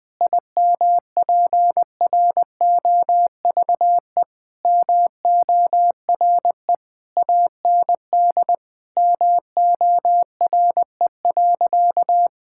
Morse code, heard as a keyed tone, about 21 wpm, 700 Hz. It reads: IMPROVE MORE AND MORE.
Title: Sentences from Top 1000 Words - Review Part 5 - 20wpm